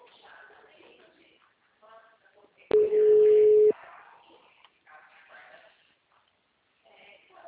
Ringback tone (Europe, 425 Hz).
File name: out- -2038- - - .74196